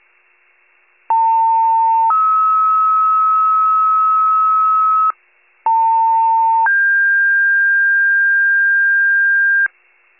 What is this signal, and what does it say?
Two-tone page, heard as a one-second tone, then a three-second tone, about 900 then 1290 Hz, about 900 then 1600 Hz, with noise. It is heard from 1.1 to 9.7 s.
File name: pager / 95 RESCUE DUTY / wcalert